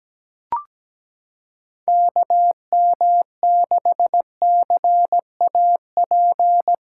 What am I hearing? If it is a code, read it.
Morse code, heard as a keyed tone, about 17 wpm, 700 Hz. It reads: KM6CAP
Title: USA Callsign DE Callsign 17wpm